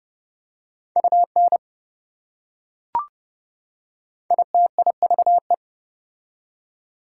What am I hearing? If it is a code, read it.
Morse code, heard as a keyed tone, about 28 wpm, 700 Hz. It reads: UN ITIVE